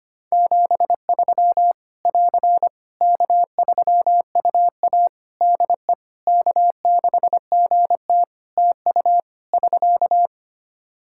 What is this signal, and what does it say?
Morse code, heard as a keyed tone, about 25 wpm, 700 Hz. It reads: 73 <AR> K3UA DE K6GT TU <SK>
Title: Practice QSO - Exchange 4 - Ending - 25wpm